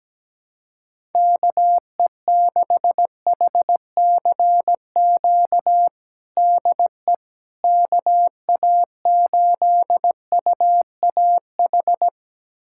Morse code, heard as a keyed tone, about 17 wpm, 700 Hz. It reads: KE6HCQ DE KA8UAH
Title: USA Callsign DE Callsign 17wpm